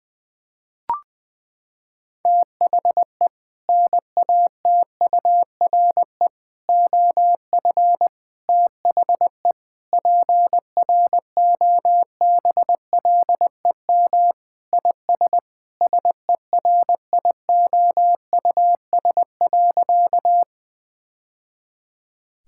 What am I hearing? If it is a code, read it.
Morse code, heard as a keyed tone, about 21 wpm, 700 Hz. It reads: THE NATURE OF THE PROBLEM IS SERIOUS.
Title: Sentences with 3-Word Phrases - 20wpm